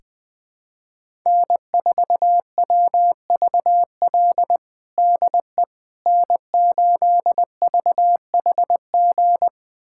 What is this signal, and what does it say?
Morse code, heard as a keyed tone, about 20 wpm, 700 Hz. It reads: N4WVL DE N8VHG
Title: USA Callsign DE Callsign - 5 Characters - 20wpm